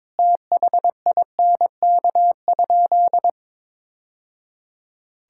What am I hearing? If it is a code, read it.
Morse code, heard as a keyed tone, about 22 wpm, 700 Hz. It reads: THINK?
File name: practice / 22wpm / Sentences with 3-Word Phrases - 22wpm